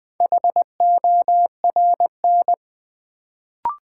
Morse code, heard as a keyed tone, about 20 wpm, 700 Hz. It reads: HORN
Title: Build A Word - 1 Letter At A Time - 4-Letter Words - 20wpm